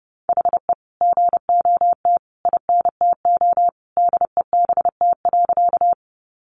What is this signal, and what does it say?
Morse code, heard as a keyed tone, about 30 wpm, 700 Hz. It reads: HE GOT INTO DEBT.
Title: Sentences from Top 1300 Words 30wpm